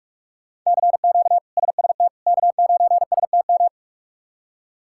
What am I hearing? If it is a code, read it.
Morse code, heard as a keyed tone, about 46 wpm, 700 Hz. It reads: CQ SST K9STM
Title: K1USN SST QSO Practice - 45wpm